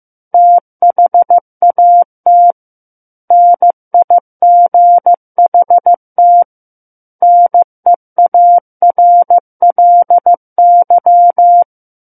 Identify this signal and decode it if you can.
Morse code, heard as a keyed tone, about 15 wpm, 700 Hz. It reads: THAT NIGHT NEARLY
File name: Book One / 06-Chapter 6 - 15 WPM